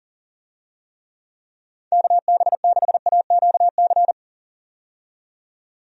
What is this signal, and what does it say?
Morse code, heard as a keyed tone, about 40 wpm, 700 Hz. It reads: KB6AQC
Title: USA Call Signs - 6 Characters - 40wpm